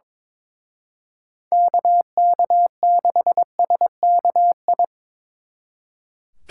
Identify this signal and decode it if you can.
Morse code, heard as a keyed tone, about 26 wpm, 700 Hz. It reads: KK6SKI
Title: Speed Racing - USA Call Signs 22wpm